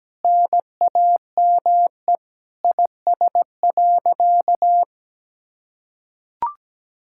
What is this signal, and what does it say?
Morse code, heard as a keyed tone, about 17 wpm, 700 Hz. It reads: NAME IS.
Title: Sentences from Top 1000 Words - Review Part 6 - 17wpm